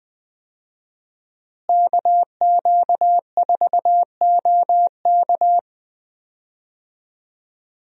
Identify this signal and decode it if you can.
Morse code, heard as a keyed tone, about 20 wpm, 700 Hz. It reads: KQ4OK
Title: USA Call Signs - 5 Characters - Repeated 3x - 20wpm